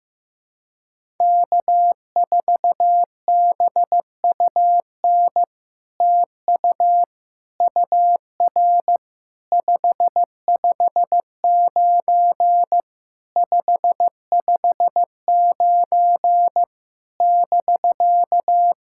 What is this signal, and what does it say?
Morse code, heard as a keyed tone, about 15 wpm, 700 Hz. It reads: K4BUN TU UR 559 559 <BK>
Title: SOTA QSO Practice 15wpm